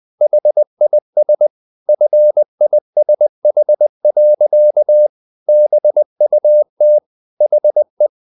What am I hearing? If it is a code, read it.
Morse code, heard as a keyed tone, about 20 wpm, 600 Hz. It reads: HIS FISH. BUT HE
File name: the old man and the sea part 2 20wpm